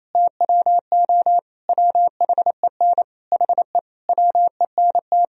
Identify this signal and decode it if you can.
Morse code, heard as a keyed tone, about 28 wpm, 700 Hz. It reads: TWO WHEN HE WENT
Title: Sentences from Top 1000 Words - Review Part 3 - 28wpm